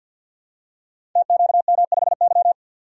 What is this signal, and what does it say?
Morse code, heard as a keyed tone, about 52 wpm, 700 Hz. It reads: TQM5Y
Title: Code Groups - 5 Letter:Numbers - 50wpm